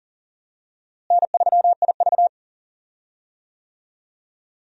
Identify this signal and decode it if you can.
Morse code, heard as a keyed tone, about 40 wpm, 700 Hz. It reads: N3IV
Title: USA Call Signs - 4 Characters - Repeated 2x - 40wpm